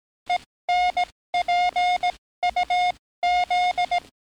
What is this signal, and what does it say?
Morse code, heard as a keyed tone, about 18 wpm, spaced as for 14 wpm, 710 Hz. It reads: ENPUZ
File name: training / NAVPERS 11049to11060 / NAVPERS 11057 RA RB